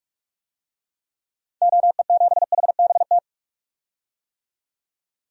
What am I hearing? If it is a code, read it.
Morse code, heard as a keyed tone, about 47 wpm, 700 Hz. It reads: OE7HBT
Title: DXCC-100 Call Signs - 6 Characters - 45wpm